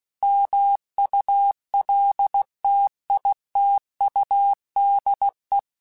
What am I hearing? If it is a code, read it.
Morse code, heard as a keyed tone, about 16 wpm, 800 Hz. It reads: MULTITUDE
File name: Technical Subjects / Technical Subjects-mp3s / OR Gate @15